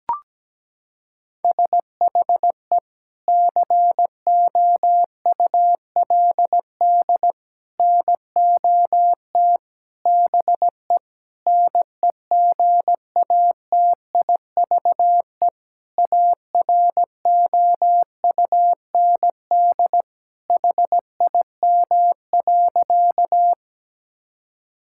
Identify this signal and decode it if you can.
Morse code, heard as a keyed tone, about 17 wpm, 700 Hz. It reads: SHE COULD NOT BE NEGATIVE AROUND HIM.
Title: Sentences from Top 1000 Words - Review Part 6 - 17wpm